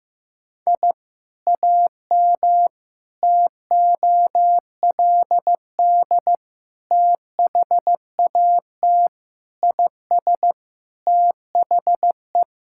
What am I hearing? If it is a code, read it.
Morse code, heard as a keyed tone, about 16 wpm, 700 Hz. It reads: I AM TOLD THAT IS THE
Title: Sentences from Top 1000 Words - Review Part 5 - 15wpm